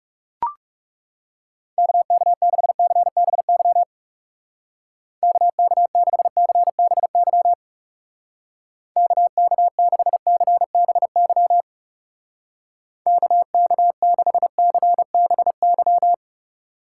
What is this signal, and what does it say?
Morse code, heard as a keyed tone, about 36 wpm, 700 Hz. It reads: KK6CBY KK6CBY KK6CBY KK6CBY
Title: Speed Racing - USA 2x3 Call Signs 30wpm